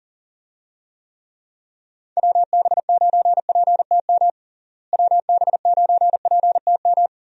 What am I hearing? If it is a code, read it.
Morse code, heard as a keyed tone, about 40 wpm, 700 Hz. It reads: WB9PTM WB9PTM
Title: USA Call Signs - 6 Characters - Repeated 3x - 40wpm